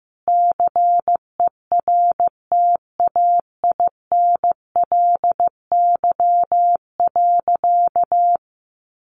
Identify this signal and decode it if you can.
Morse code, heard as a keyed tone, about 15 wpm, 700 Hz. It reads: CERTAINLY.
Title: Sentences from Top 900 Words 15wpm